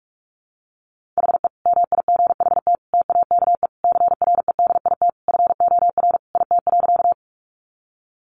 Morse code, heard as a keyed tone, about 44 wpm, 700 Hz. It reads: HE MIGHT TAKE CREDIT FOR IT.
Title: Sentences from Top 1000 Words 45wpm